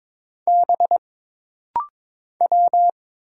Morse code, heard as a keyed tone, about 22 wpm, 700 Hz. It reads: B W
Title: Single Letters - Rapid-Fire 22wpm